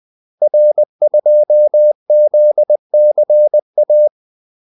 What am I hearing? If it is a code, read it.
Morse code, heard as a keyed tone, about 20 wpm, 600 Hz. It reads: R2ZCA
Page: Kaldesignaler 20 wpm